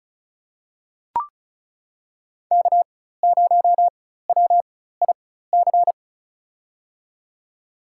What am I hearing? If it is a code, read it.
Morse code, heard as a keyed tone, about 36 wpm, spaced as for 16 wpm, 700 Hz. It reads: K0WIC
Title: USA Call Signs - 5 Characters - 4x Character Spacing - 35wpm